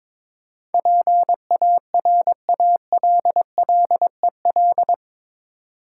Morse code, heard as a keyed tone, about 22 wpm, 700 Hz. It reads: PARALLEL
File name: practice / 22wpm / Eight Letter Words - 22wpm